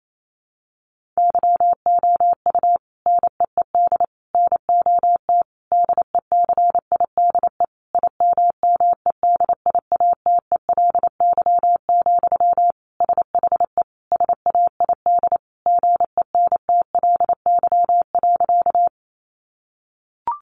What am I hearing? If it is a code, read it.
Morse code, heard as a keyed tone, about 28 wpm, 700 Hz. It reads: YOU NEED NOT DECIDE IMMEDIATELY, SHE SAID GENTLY.
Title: Sentences from Top 1500 Words 28wpm